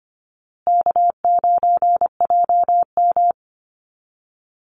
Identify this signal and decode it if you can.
Morse code, heard as a keyed tone, about 24 wpm, 700 Hz. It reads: K9JM
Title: Easy Call Signs Repeated 3x - 25wpm